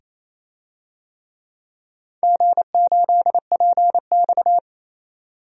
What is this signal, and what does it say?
Morse code, heard as a keyed tone, about 27 wpm, 700 Hz. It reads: G8PX
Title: DXCC-100 Call Signs - 4 Characters - 28wpm